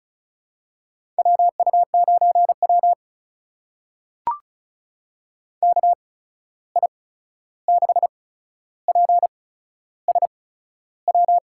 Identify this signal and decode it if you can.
Morse code, heard as a keyed tone, about 36 wpm, spaced as for 13 wpm, 700 Hz. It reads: WU9W KI6PSW
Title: USA Call Signs - 8x Character Spacing - 35wpm